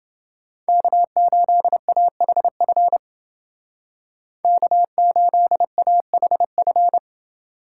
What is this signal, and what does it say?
Morse code, heard as a keyed tone, about 28 wpm, 700 Hz. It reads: K8AHF K8AHF